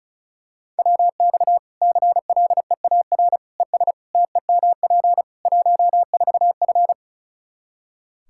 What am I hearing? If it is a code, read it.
Morse code, heard as a keyed tone, about 36 wpm, 700 Hz. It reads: WX CLEAR ES TEMP 14F
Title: K1USN SST QSO Practice 35wpm